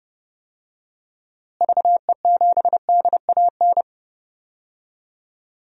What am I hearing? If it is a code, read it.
Morse code, heard as a keyed tone, about 31 wpm, 700 Hz. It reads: VE7DAN
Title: DXCC-100 Call Signs - 6 Characters - Repeated 2x - 30wpm